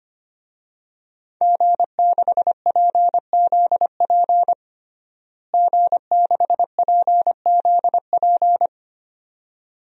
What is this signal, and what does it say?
Morse code, heard as a keyed tone, about 25 wpm, 700 Hz. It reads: G6PZP G6PZP
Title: DXCC-100 Call Signs - 5 Characters - Repeated 2x - 25wpm